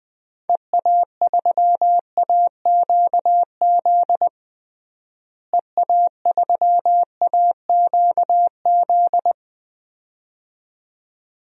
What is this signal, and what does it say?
Morse code, heard as a keyed tone, about 21 wpm, 700 Hz. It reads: EA3AQZ EA3AQZ
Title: DXCC-100 Call Signs - 6 Characters - Repeated 2x - 20wpm